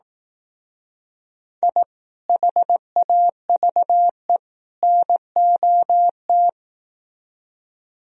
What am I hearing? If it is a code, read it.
Morse code, heard as a keyed tone, about 18 wpm, 700 Hz. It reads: I HAVE NOT
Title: Speed-Racing - Sets of 3 Words 15wpm